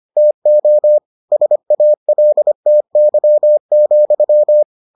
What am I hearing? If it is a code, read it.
Morse code, heard as a keyed tone, about 25 wpm, 600 Hz. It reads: TO SALTY,